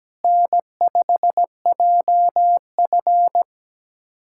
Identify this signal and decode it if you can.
Morse code, heard as a keyed tone, about 17 wpm, 700 Hz. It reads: N5JF
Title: USA Callsign DE Callsign - 4 Characters - 17wpm